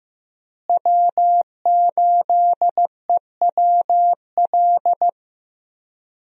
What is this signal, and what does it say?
Morse code, heard as a keyed tone, about 15 wpm, 700 Hz. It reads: W8EWL
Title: USA Call Signs - 5 Characters - Repeated 3x - 15wpm